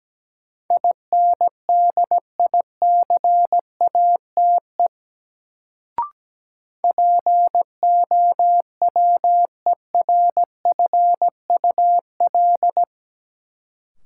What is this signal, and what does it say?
Morse code, heard as a keyed tone, about 17 wpm, 700 Hz. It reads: INDICATE POWERFUL
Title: Sentences from Top 1300 Words - New Word Drill Alt - 17wpm